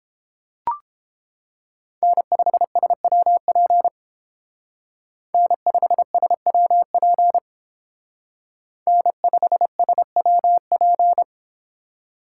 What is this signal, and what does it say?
Morse code, heard as a keyed tone, about 29 wpm, 700 Hz. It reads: N5SWP N5SWP N5SWP